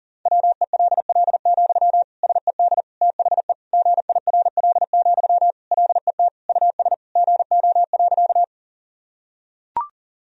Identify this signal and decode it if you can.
Morse code, heard as a keyed tone, about 37 wpm, 700 Hz. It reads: WELL, SED THE GIRL, LET US GO.
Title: Sentences from Top 400 Words and Abbreviations 40wpm